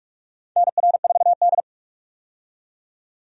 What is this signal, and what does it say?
Morse code, heard as a keyed tone, about 46 wpm, 700 Hz. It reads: NR4D
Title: USA Call Signs - 4 Characters - 45wpm